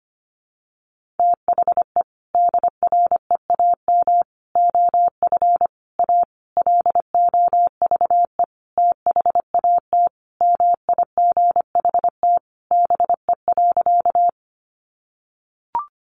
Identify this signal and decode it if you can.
Morse code, heard as a keyed tone, about 25 wpm, 700 Hz. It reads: THE DREAM OF A LOVE THAT MIGHT BE.
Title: Sentences from Top 1500 Words 25wpm